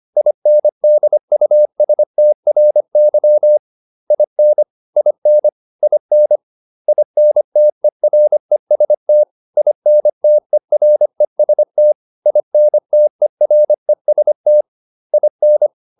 Morse code, heard as a keyed tone, about 25 wpm, 600 Hz. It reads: INDUSTRY IN IN IN INTEREST INTEREST INTEREST IN